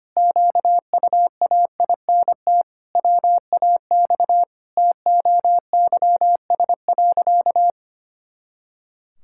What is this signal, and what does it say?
Morse code, heard as a keyed tone, about 25 wpm, 700 Hz. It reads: QUAINT WAX TOYS.
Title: Pangram Challenge v2 - 25wpm